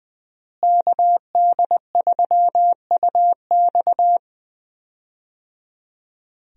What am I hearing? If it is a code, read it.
Morse code, heard as a keyed tone, about 21 wpm, 700 Hz. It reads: KD3UX
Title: USA Call Signs - 5 Characters - Repeated 3x - 20wpm